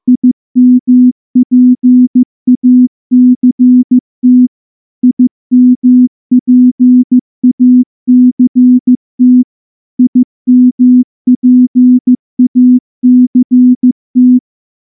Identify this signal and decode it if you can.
Morse code, heard as a keyed tone, about 14 wpm, 250 Hz. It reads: IMPACT IMPACT IMPACT
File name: morse-training-15WPM